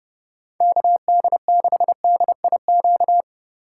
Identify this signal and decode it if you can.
Morse code, heard as a keyed tone, about 29 wpm, 700 Hz. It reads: KD6DIQ